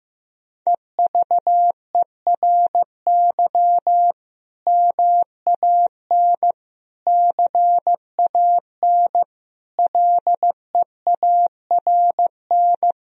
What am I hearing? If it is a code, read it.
Morse code, heard as a keyed tone, about 15 wpm, 700 Hz. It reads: EVERY MAN CAN LEARN
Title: Sentences from Top 1000 Words - Review Part 1 - 15wpm